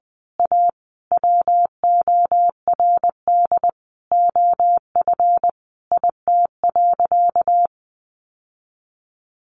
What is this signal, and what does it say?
Morse code, heard as a keyed tone, about 20 wpm, 700 Hz. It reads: A WORD OF IT.
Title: Sentences from Top 1200 Words 20wpm